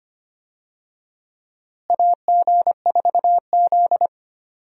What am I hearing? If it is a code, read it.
Morse code, heard as a keyed tone, about 25 wpm, 700 Hz. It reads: AG4Z
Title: USA Call Signs - 4 Characters - Repeated 2x - 25wpm